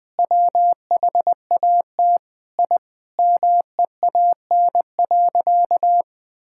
Morse code, heard as a keyed tone, about 20 wpm, 700 Hz. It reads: WHAT I MEAN.
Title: Sentences with 4-Word Phrases - 20wpm